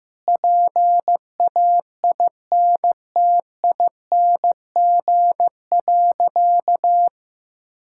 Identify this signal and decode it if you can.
Morse code, heard as a keyed tone, about 15 wpm, 700 Hz. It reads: PAINTING.
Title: Sentences from Top 1000 Words - Review Part 6 - 15wpm